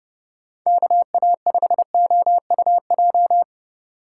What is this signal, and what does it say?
Morse code, heard as a keyed tone, about 32 wpm, 700 Hz. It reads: KA5OUJ
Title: USA Call Signs - 4x Character Spacing - 30wpm